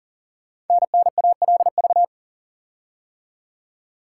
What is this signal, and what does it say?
Morse code, heard as a keyed tone, about 40 wpm, 690 Hz. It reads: NNALV
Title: Code Groups - 5 Letter:Numbers - 40wpm